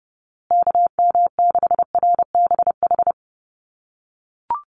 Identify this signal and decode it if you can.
Morse code, heard as a keyed tone, about 30 wpm, 700 Hz. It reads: KM6RBH
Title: USA Call Signs Repeated 3x 30wpm